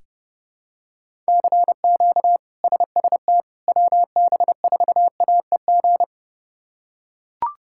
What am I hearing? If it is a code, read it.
Morse code, heard as a keyed tone, about 30 wpm, 700 Hz. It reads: CQ SST WB4AEG
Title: K1USN SST QSO Practice - 30wpm